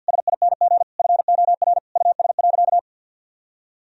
Morse code, heard as a keyed tone, about 50 wpm, 700 Hz. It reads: SING FOR US.
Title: Sentences from Top 1000 Words - Review Part 2 - 50wpm